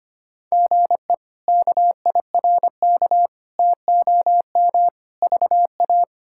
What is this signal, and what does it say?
Morse code, heard as a keyed tone, about 25 wpm, 700 Hz. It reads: GE KIRK TOM VA
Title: K1USN SST QSO Practice - 25wpm